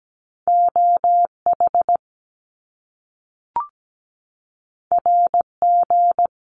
Morse code, heard as a keyed tone, about 17 wpm, 700 Hz. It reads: OH RG
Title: Top Two Letter Combinations in Words 17wpm